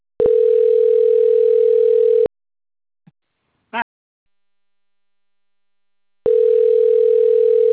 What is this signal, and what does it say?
Ringback tone (North America), 440 + 480 Hz.